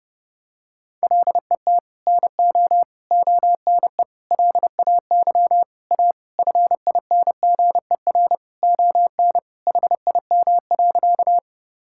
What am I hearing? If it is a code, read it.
Morse code, heard as a keyed tone, about 32 wpm, 700 Hz. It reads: LET NO ONE LAY A FINGER ON HIM.
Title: Sentences from Top 1600 Words 30wpm